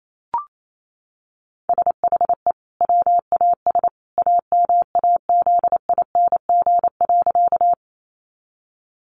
Morse code, heard as a keyed tone, about 28 wpm, 700 Hz. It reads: SHE WAS AMAZING.
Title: Sentences from Top 1500 Words 28wpm